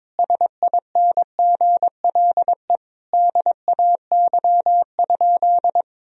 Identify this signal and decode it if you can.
Morse code, heard as a keyed tone, about 21 wpm, 700 Hz. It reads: SINGLE DAY?
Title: Sentences from Top 1000 Words - Review Part 4 - 22wpm